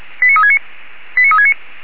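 Five-tone selective call (ZVEI), heard as quick runs of five tones, tones of about 80 ms. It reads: 89369 89369
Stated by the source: ZVEI five-tone selective call received on an SDR by Feuerwehr Orlamünde.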